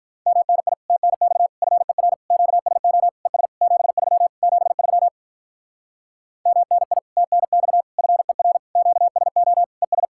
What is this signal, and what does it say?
Morse code, heard as a keyed tone, about 48 wpm, 700 Hz. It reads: MNI TNX FER QSO ES 73 73 MNI TNX FER QSO ES